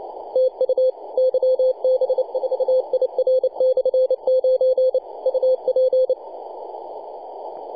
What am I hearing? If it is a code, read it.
Morse code, heard as a keyed tone, about 29 wpm, 520 Hz, with radio noise. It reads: TU YB4IR/9 UP
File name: posts / 2015-june-21-0848z-yb4ir-9-14-010mhz-cw